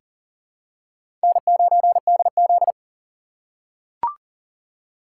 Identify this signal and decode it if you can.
Morse code, heard as a keyed tone, about 40 wpm, 700 Hz. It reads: N9DZ
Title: USA Call Signs - 6x Character Spacing - 40wpm